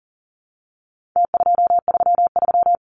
Morse code, heard as a keyed tone, about 40 wpm, 700 Hz. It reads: T233
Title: Numbers - Contest Style - 4-Digit 40wpm